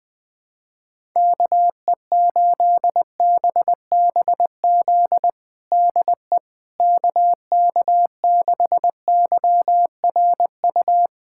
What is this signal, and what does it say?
Morse code, heard as a keyed tone, about 20 wpm, 700 Hz. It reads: KE8BBZ DE KK6YRU
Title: USA Callsign DE Callsign - 6 Characters - 20wpm